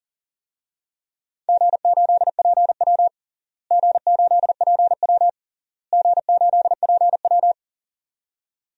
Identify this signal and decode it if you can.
Morse code, heard as a keyed tone, about 40 wpm, 700 Hz. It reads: G8PW G8PW G8PW
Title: DXCC-100 Call Signs - 4 Characters - Repeated 3x - 40wpm